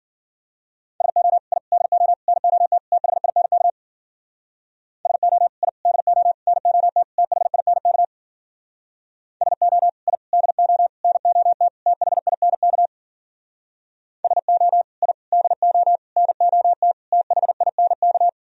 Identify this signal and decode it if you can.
Morse code, heard as a keyed tone, about 52 wpm, 700 Hz. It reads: SO I DO NOT THINK SO I DO NOT THINK SO I DO NOT THINK SO I DO NOT THINK